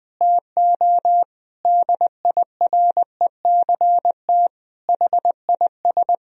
Morse code, heard as a keyed tone, about 21 wpm, 700 Hz. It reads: TO DIRECT HIS
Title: Sentences from Top 1000 Words - Review Part 2 - 20wpm